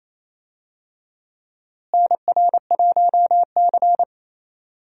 Morse code, heard as a keyed tone, about 28 wpm, 700 Hz. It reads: NR1C
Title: USA Call Signs - 4 Characters - Repeated 3x - 28wpm